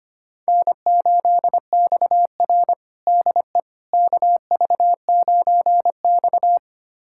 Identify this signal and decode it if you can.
Morse code, heard as a keyed tone, about 25 wpm, 700 Hz. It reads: N8XR DE KV9X
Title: USA Callsign DE Callsign - 4 Characters - 25wpm